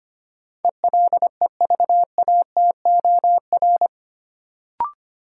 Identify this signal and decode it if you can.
Morse code, heard as a keyed tone, about 25 wpm, 700 Hz. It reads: ELEVATOR
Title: Sentences from Top 1600 Words - New Word Drill Alt - 25wpm